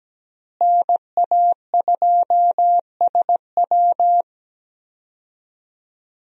Morse code, heard as a keyed tone, about 17 wpm, 700 Hz. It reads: NA2SW
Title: USA Call Signs - 5 Characters - Repeated 3x - 17wpm